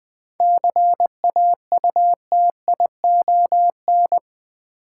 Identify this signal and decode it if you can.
Morse code, heard as a keyed tone, about 20 wpm, 700 Hz. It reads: CAUTION
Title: 7 Letter Words - First 3 Letters Given - 20wpm